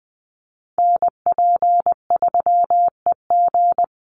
Morse code, heard as a keyed tone, about 20 wpm, 700 Hz. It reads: NP3EG
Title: USA Call Signs 20wpm